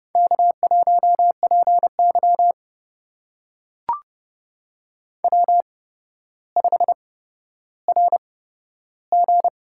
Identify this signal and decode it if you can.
Morse code, heard as a keyed tone, about 30 wpm, spaced as for 13 wpm, 700 Hz. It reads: K1PY W5RG
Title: USA Call Signs - 4 Characters - 8x Character Spacing - 30wpm